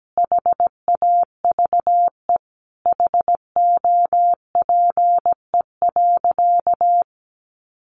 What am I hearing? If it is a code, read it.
Morse code, heard as a keyed tone, about 17 wpm, 700 Hz. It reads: HAVE HOPE.
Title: Sentences from Top 1000 Words 17wpm